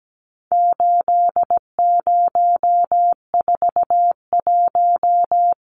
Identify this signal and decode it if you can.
Morse code, heard as a keyed tone, about 17 wpm, 700 Hz. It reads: 8041
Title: Numbers - 4-Digit 17wpm